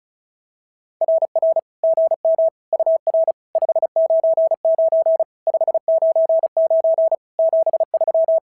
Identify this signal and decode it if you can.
Morse code, heard as a keyed tone, about 33 wpm, 660 Hz. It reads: RR GM UR 599 599 73
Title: SOTA QSO Practice 35wpm